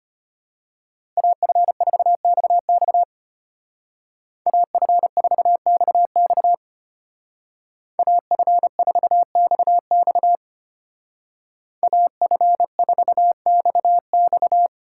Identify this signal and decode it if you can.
Morse code, heard as a keyed tone, about 31 wpm, 700 Hz. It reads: AF4XX AF4XX AF4XX AF4XX